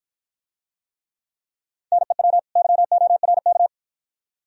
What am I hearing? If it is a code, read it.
Morse code, heard as a keyed tone, about 51 wpm, 700 Hz. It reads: NEW YORK